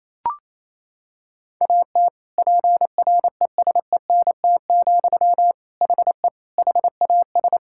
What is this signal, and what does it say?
Morse code, heard as a keyed tone, about 28 wpm, 700 Hz. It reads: AT PRESENT, HE HAS
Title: Sentences from Top 1000 Words - Review Part 5 - 28wpm